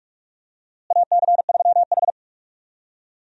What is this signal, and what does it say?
Morse code, heard as a keyed tone, about 44 wpm, 700 Hz. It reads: AC3H
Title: USA Call Signs - 4 Characters - Repeated 2x - 45wpm